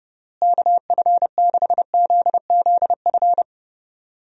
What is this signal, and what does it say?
Morse code, heard as a keyed tone, about 30 wpm, 700 Hz. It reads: KF6ZZF
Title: USA Call Signs - 6 Characters - Repeated 3x - 30wpm